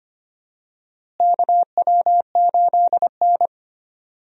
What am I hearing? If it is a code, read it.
Morse code, heard as a keyed tone, about 25 wpm, 700 Hz. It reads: KW8N
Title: USA Call Signs - 4 Characters - Repeated 2x - 25wpm